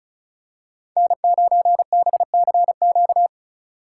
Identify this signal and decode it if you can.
Morse code, heard as a keyed tone, about 36 wpm, 700 Hz. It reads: N9BCQ
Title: USA Call Signs - 5 Characters - 2x Character Spacing - 35wpm